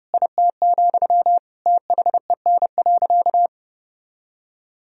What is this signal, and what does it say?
Morse code, heard as a keyed tone, about 30 wpm, 700 Hz. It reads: IT, THEN.
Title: Sentences from Top 200 Words and Abbreviations 30wpm